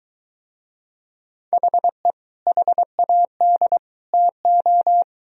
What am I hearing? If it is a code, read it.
Morse code, heard as a keyed tone, about 21 wpm, 700 Hz. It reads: HE HAD TO